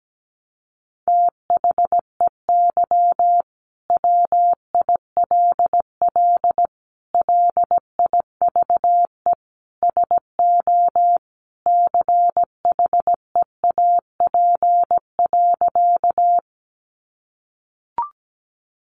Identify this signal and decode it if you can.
Morse code, heard as a keyed tone, about 17 wpm, 700 Hz. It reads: THEY WILL LIVE SO CHEAP.
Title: Sentences from Top 1200 Words 17wpm